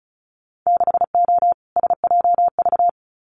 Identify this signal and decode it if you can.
Morse code, heard as a keyed tone, about 36 wpm, 700 Hz. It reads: 6O SJV
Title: Winter Field Day Exchanges Encore 35wpm